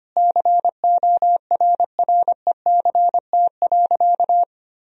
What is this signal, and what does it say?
Morse code, heard as a keyed tone, about 25 wpm, 700 Hz. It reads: CORRECT.
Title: Sentences from Top 1000 Words - Review Part 2 - 25wpm